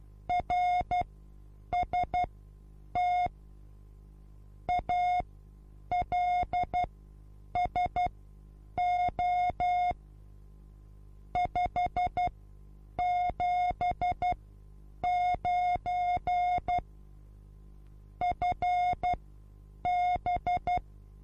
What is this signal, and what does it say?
Morse code, heard as a keyed tone, about 12 wpm, spaced as for 8 wpm, 720 Hz. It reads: RST ALSO 579 FB
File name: Onlinelehrgang / morsen / DJ4UF Morsen23